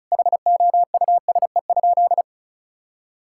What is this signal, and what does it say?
Morse code, heard as a keyed tone, about 36 wpm, 700 Hz. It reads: HOUSE?
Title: Sentences from Top 1000 Words - Review Part 6 - 35wpm